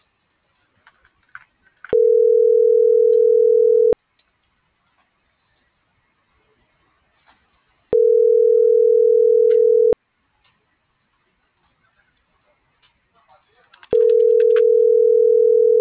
Ringback tone (North America), 440 + 480 Hz.